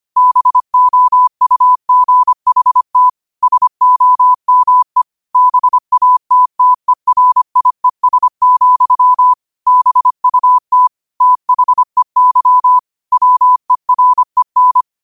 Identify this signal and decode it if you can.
Morse code, heard as a keyed tone, about 25 wpm, 1000 Hz. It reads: BOUGHT SOME BATTERIES, BUT THEY WEREN
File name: qotdfiles / 25wpm